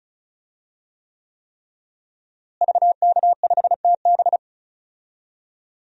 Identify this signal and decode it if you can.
Morse code, heard as a keyed tone, about 36 wpm, 700 Hz. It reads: VK5TB
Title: DXCC-100 Call Signs - 5 Characters - Repeated 3x - 35wpm